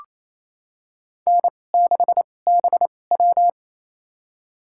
Morse code, heard as a keyed tone, about 28 wpm, spaced as for 20 wpm, 700 Hz. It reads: N6BW
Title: USA Call Signs - 4 Characters - 2x Character Spacing - 28wpm